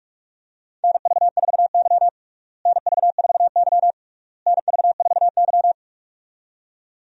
Morse code, heard as a keyed tone, about 42 wpm, 700 Hz. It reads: NV4Y NV4Y NV4Y
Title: USA Call Signs - 4 Characters - Repeated 3x - 45wpm